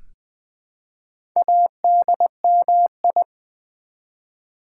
Morse code, heard as a keyed tone, about 20 wpm, 700 Hz. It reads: ADMI
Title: Build A Word - 1 Letter At A Time - 3 to 5-Letter Words - 20wpm